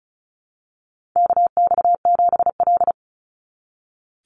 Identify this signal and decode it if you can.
Morse code, heard as a keyed tone, about 36 wpm, 700 Hz. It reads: KX7L
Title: NA SOTA Chaser Callsigns 35wpm